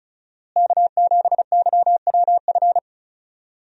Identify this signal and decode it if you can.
Morse code, heard as a keyed tone, about 35 wpm, 700 Hz. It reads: K7YWF
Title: USA Call Signs - 5 Characters - Repeated 2x - 35wpm